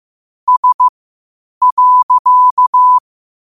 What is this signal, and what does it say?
Morse code, heard as a keyed tone, about 15 wpm, spaced as for 8 wpm, 1000 Hz. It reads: S.
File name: qotdfiles / 8wpm